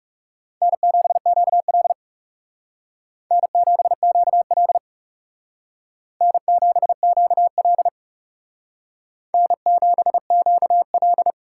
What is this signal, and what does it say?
Morse code, heard as a keyed tone, about 38 wpm, 700 Hz. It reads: N7QL N7QL N7QL N7QL